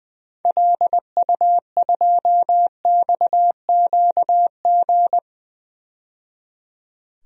Morse code, heard as a keyed tone, about 20 wpm, 700 Hz. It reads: LU2XQG
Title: DXCC-100 Call Signs - 6 Characters - 20wpm